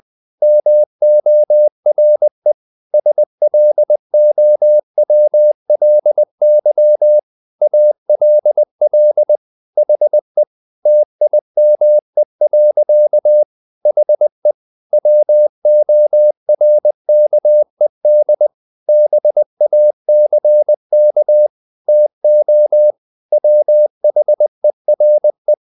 Morse code, heard as a keyed tone, about 20 wpm, 600 Hz. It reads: MORE SLOWLY ALL HE TIME. HE WORKED BACK TO WHERE
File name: the old man and the sea part 2 20wpm